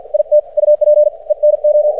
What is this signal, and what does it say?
Morse code, heard as a keyed tone, about 53 wpm, spaced as for 46 wpm, 600 Hz. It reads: IT UP ENO
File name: archived / CRSnet